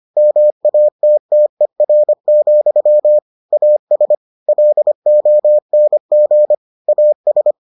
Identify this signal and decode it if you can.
Morse code, heard as a keyed tone, about 25 wpm, 600 Hz. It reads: MATTER, AS LONG AS